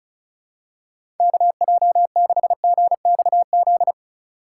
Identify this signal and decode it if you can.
Morse code, heard as a keyed tone, about 36 wpm, 700 Hz. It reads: KJ6GXZ
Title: USA Call Signs - 6 Characters - 2x Character Spacing - 35wpm